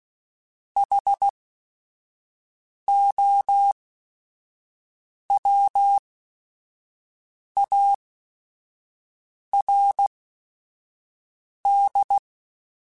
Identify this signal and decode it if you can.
Morse code, heard as a keyed tone, about 16 wpm, spaced as for 5 wpm, 800 Hz. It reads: HOWARD